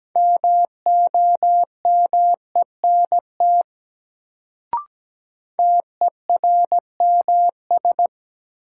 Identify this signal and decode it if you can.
Morse code, heard as a keyed tone, about 17 wpm, 700 Hz. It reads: MOMENT TERMS
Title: Sets of 3 Words - New Word Drill - 17wpm